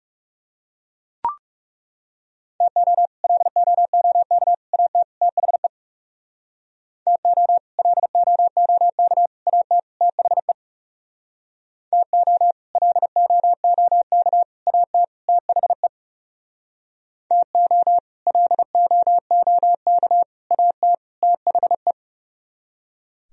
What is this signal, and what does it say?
Morse code, heard as a keyed tone, about 37 wpm, 700 Hz. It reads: TO LOOK AT THE TO LOOK AT THE TO LOOK AT THE TO LOOK AT THE
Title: Speed-Racing - Sets of 4 Words 30wpm